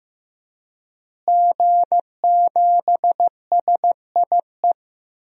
Morse code, heard as a keyed tone, about 15 wpm, 700 Hz. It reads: G7SIE
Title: DXCC-100 Call Signs - 5 Characters - Repeated 2x - 15wpm